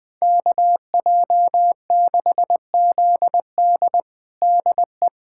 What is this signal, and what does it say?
Morse code, heard as a keyed tone, about 21 wpm, 700 Hz. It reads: KJ6ZD DE
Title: USA Callsign DE Callsign - 5 Characters - 20wpm